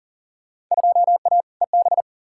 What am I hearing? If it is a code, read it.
Morse code, heard as a keyed tone, about 40 wpm, 700 Hz. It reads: 2A EB
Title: Field Day Exchanges - 40wpm